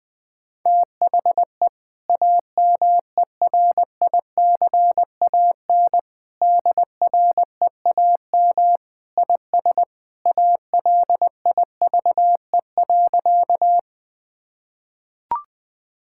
Morse code, heard as a keyed tone, about 20 wpm, 700 Hz. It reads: THE AMERICAN DREAM IS ALIVE.
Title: Sentences with 2-Word Phrases - 20wpm